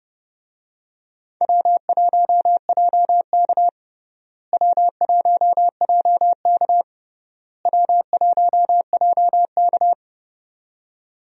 Morse code, heard as a keyed tone, about 30 wpm, 700 Hz. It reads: W1JK W1JK W1JK
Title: USA Call Signs - 4 Characters - Repeated 3x - 30wpm